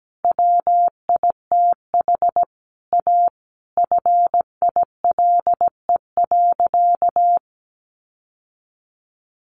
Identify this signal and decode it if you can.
Morse code, heard as a keyed tone, about 17 wpm, 700 Hz. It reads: WITH A FILE.
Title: Sentences from Top 1200 Words 17wpm